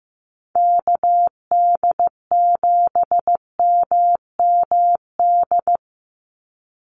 Morse code, heard as a keyed tone, about 15 wpm, 700 Hz. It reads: KD7MMD
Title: USA Call Signs 15wpm